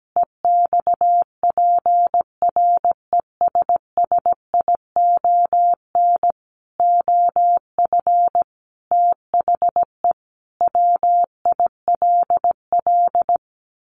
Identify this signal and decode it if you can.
Morse code, heard as a keyed tone, about 17 wpm, 700 Hz. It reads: EXPRESSION OF THE WILL
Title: Sentences from Top 1500 Words 17wpm